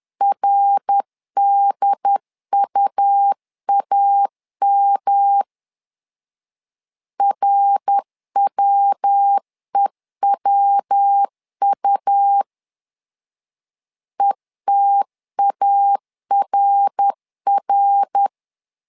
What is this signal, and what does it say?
Morse code, heard as a keyed tone, about 10 wpm, 800 Hz. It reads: RDUAM RWEWU ETARR
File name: LEZIONI / 3 MWRO / 3 MWR 10wpm 1952